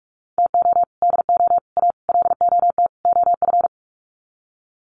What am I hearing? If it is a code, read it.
Morse code, heard as a keyed tone, about 44 wpm, 700 Hz. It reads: TO DO A LOT OF
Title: Sets of 5 Words Encore 45wpm